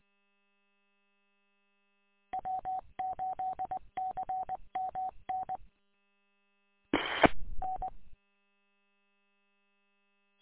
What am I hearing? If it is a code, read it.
Morse code, heard as a keyed tone, about 20 wpm, 750 Hz. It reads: W8CMN EE N
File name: w8cmn- -combined